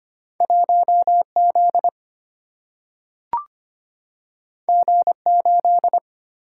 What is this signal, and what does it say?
Morse code, heard as a keyed tone, about 25 wpm, 700 Hz. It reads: 1 Z G 8
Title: Single-Letter Single-Number 25wpm